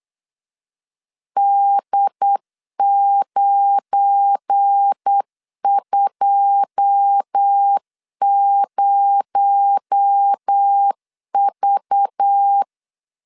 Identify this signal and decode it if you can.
Morse code, heard as a keyed tone, about 8 wpm, 800 Hz. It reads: D920V
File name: LEZIONI / 9 2 8wpm 3437